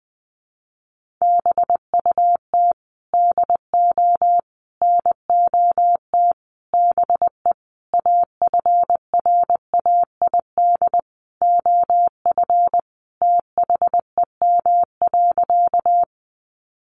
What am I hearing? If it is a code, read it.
Morse code, heard as a keyed tone, about 20 wpm, 700 Hz. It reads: BUT DO NOT BE AFRAID OF THEM.
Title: Sentences from Top 1500 Words 20wpm